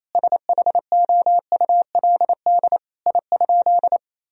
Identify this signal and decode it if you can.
Morse code, heard as a keyed tone, about 28 wpm, 700 Hz. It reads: SHOULD I?
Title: Sentences from Top 1000 Words - Review Part 1 - 28wpm